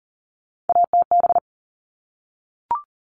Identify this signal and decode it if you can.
Morse code, heard as a keyed tone, about 40 wpm, 700 Hz. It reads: ATB
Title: All Cut Numbers - 3-Digit 40wpm